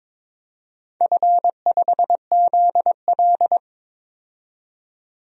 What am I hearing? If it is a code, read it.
Morse code, heard as a keyed tone, about 22 wpm, 690 Hz. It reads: F5ZL
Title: DXCC-100 Call Signs - 4 Characters - 22wpm